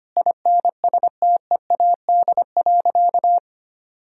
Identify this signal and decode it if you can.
Morse code, heard as a keyed tone, about 25 wpm, 700 Hz. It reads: INSTEAD.
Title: Sentences from Top 1000 Words - Review Part 4 - 25wpm